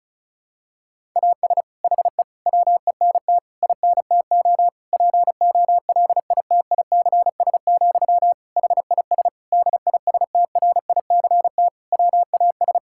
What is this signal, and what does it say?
Morse code, heard as a keyed tone, about 36 wpm, 700 Hz. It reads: AS HE WENT INTO POLITICS, HIS DISTRICT WAS
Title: Sentences from Top 1600 Words 35wpm